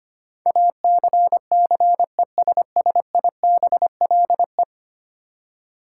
Morse code, heard as a keyed tone, about 25 wpm, 700 Hz. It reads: ACCESSIBLE
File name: 10 Letter Words - First 3 Letters Given - 25wpm